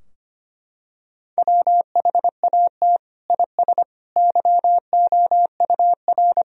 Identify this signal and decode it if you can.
Morse code, heard as a keyed tone, about 25 wpm, 700 Hz. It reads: WHAT IS YOUR
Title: Sentences from Top 1000 Words - Review Part 4 - 25wpm